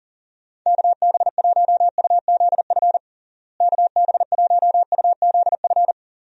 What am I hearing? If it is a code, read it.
Morse code, heard as a keyed tone, about 40 wpm, 700 Hz. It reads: KB1UZF KB1UZF
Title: USA Call Signs - 6 Characters - Repeated 2x - 40wpm